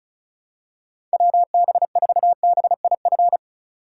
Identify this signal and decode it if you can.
Morse code, heard as a keyed tone, about 34 wpm, 700 Hz. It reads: WB4BIF
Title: USA Call Signs - 6x Character Spacing - 35wpm